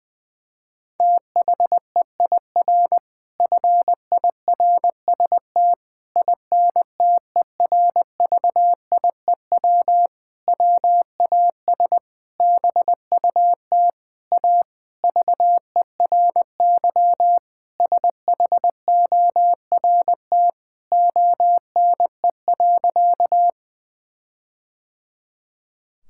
Morse code, heard as a keyed tone, about 21 wpm, 700 Hz. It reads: THEIR FIRST INTERVIEW WAS BUT A VERY SHORT ONE.
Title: Sentences from Top 1600 Words 20wpm